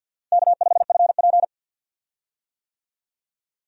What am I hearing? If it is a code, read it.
Morse code, heard as a keyed tone, about 52 wpm, 700 Hz. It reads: K5FP
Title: USA Call Signs - 4 Characters - 50wpm